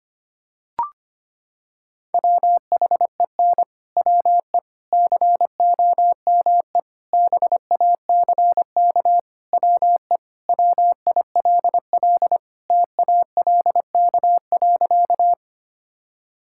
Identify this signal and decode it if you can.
Morse code, heard as a keyed tone, about 25 wpm, 700 Hz. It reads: WHEN WE COME BACK WE WILL TALK.
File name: practice / 25wpm / Sentences with 4-Word Phrases - 25wpm